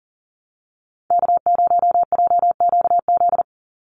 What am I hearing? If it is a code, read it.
Morse code, heard as a keyed tone, about 40 wpm, 700 Hz. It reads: K0JQZ
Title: NA SOTA Activator Callsigns 40wpm